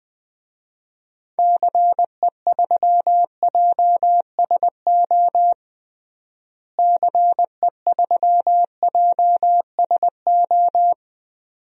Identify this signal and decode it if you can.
Morse code, heard as a keyed tone, about 20 wpm, 700 Hz. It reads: CE3JSO CE3JSO
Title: DXCC-100 Call Signs - 6 Characters - Repeated 2x - 20wpm